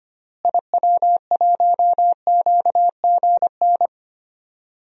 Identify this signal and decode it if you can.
Morse code, heard as a keyed tone, about 25 wpm, 700 Hz. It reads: IW1QGN
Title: DXCC-100 Call Signs - 6 Characters - Repeated 2x - 25wpm